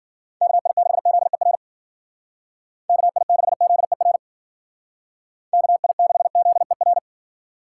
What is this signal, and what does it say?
Morse code, heard as a keyed tone, about 48 wpm, 700 Hz. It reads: KI6ZER KI6ZER KI6ZER
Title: Speed Racing - USA 2x3 Call Signs 40wpm